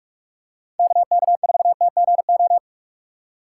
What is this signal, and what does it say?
Morse code, heard as a keyed tone, about 46 wpm, 700 Hz. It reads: KK4TGO
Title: USA Call Signs - 6 Characters - 2x Character Spacing - 45wpm